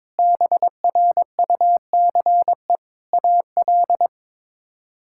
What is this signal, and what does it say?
Morse code, heard as a keyed tone, about 22 wpm, 700 Hz. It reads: BRUCE AL
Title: K1USN SST QSO Practice - 22wpm